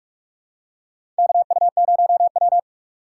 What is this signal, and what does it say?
Morse code, heard as a keyed tone, about 46 wpm, 700 Hz. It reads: KU0W
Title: USA Call Signs - 4 Characters - 2x Character Spacing - 45wpm